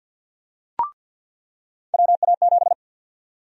Morse code, heard as a keyed tone, about 46 wpm, 700 Hz. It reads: WA7
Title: USA Call Sign Prefixes - 50wpm